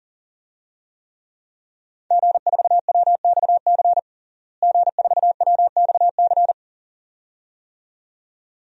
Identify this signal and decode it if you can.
Morse code, heard as a keyed tone, about 40 wpm, 700 Hz. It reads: G4WXC G4WXC
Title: DXCC-100 Call Signs - 5 Characters - Repeated 2x - 40wpm